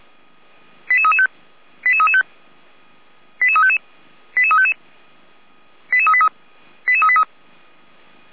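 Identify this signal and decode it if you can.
Five-tone selective call (ZVEI), heard as quick runs of five tones, tones of about 80 ms. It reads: 88335 88335 88355 88355 88383 88383
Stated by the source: ZVEI five-tone selective call received with an SDR.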